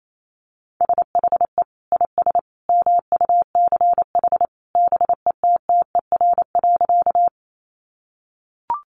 Morse code, heard as a keyed tone, about 28 wpm, 700 Hz. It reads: SHE IS MUCH BETTER.
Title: Sentences from Top 400 Words 28wpm